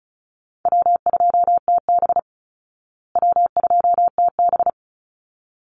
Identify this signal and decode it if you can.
Morse code, heard as a keyed tone, about 34 wpm, 700 Hz. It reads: W2TB W2TB
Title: Easy Call Signs Repeated 2x - 35wpm